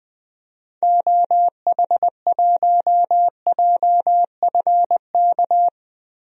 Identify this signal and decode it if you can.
Morse code, heard as a keyed tone, about 21 wpm, 700 Hz. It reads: OH1JFK
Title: DXCC-100 Call Signs - 6 Characters - 20wpm